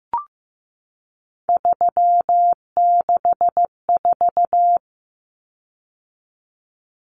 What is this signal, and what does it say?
Morse code, heard as a keyed tone, about 15 wpm, 700 Hz. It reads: 364
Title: Common Cut Numbers - 3-Digit 15wpm